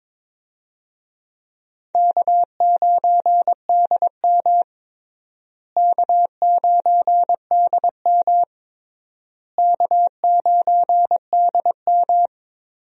Morse code, heard as a keyed tone, about 22 wpm, 700 Hz. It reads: K9DM K9DM K9DM
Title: USA Call Signs - 4 Characters - Repeated 3x - 22wpm